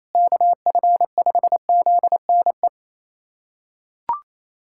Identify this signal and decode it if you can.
Morse code, heard as a keyed tone, about 28 wpm, 700 Hz. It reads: KF5ZNE
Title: USA Callsign DE Callsign 28wpm